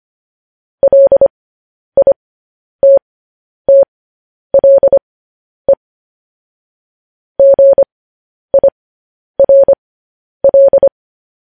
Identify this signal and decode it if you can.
Morse code, heard as a keyed tone, about 25 wpm, spaced as for 10 wpm, 550 Hz. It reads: LITTLE GIRL
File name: podcast / News headlines-1x-25x10